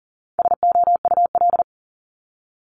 Morse code, heard as a keyed tone, about 40 wpm, 700 Hz. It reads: SOUL
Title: Multi-letter Phonemes Repeated 3x 40wpm